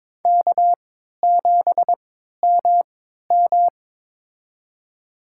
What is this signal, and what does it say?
Morse code, heard as a keyed tone, about 21 wpm, spaced as for 12 wpm, 700 Hz. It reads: K7MM
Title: USA Call Signs - 4 Characters - 3x Character Spacing - 22wpm